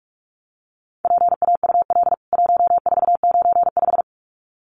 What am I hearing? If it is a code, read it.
Morse code, heard as a keyed tone, about 44 wpm, 700 Hz. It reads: PAUL 1495
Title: CWT Exchange 45wpm